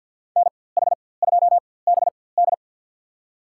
Morse code, heard as a keyed tone, about 47 wpm, spaced as for 23 wpm, 700 Hz. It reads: NH2BD
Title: USA Call Signs - 5 Characters - 4x Character Spacing - 50wpm